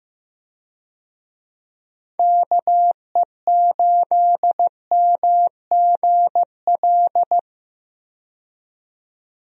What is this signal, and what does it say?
Morse code, heard as a keyed tone, about 15 wpm, 700 Hz. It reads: KE8MGL
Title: USA Call Signs - 6 Characters - 15wpm